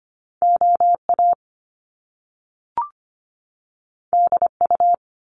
Morse code, heard as a keyed tone, about 25 wpm, 700 Hz. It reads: OA DU
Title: Top Two Letter Combinations in Words 25wpm